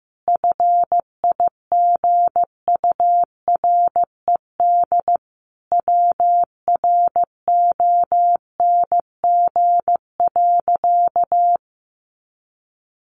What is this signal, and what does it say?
Morse code, heard as a keyed tone, about 15 wpm, 700 Hz. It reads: FIGURED WRONG.
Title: A Sound Of Thunder 15wpm